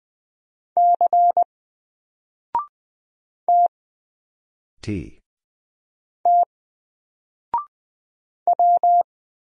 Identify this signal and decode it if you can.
Morse code, heard as a keyed tone, about 20 wpm, 700 Hz. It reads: C T T W
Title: Single Letter-Number Alt 20wpm